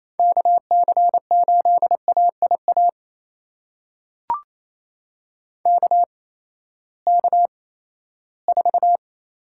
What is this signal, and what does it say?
Morse code, heard as a keyed tone, about 28 wpm, spaced as for 16 wpm, 700 Hz. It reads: KC8AIA KK4
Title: USA Call Signs - 6 Characters - 8x Character Spacing - 28wpm